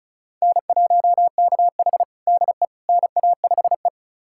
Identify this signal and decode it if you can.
Morse code, heard as a keyed tone, about 34 wpm, 700 Hz. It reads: N1KH DE NA5E
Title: USA Callsign DE Callsign - 4 Characters - 35wpm